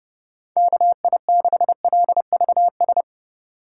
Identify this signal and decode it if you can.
Morse code, heard as a keyed tone, about 30 wpm, 700 Hz. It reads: KI6LVS
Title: USA Call Signs - 6 Characters - 4x Character Spacing - 30wpm